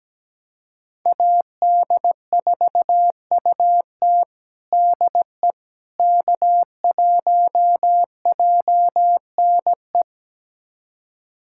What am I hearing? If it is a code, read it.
Morse code, heard as a keyed tone, about 17 wpm, 700 Hz. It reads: AD4UT DE K1JNE
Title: USA Callsign DE Callsign - 5 Characters - 17wpm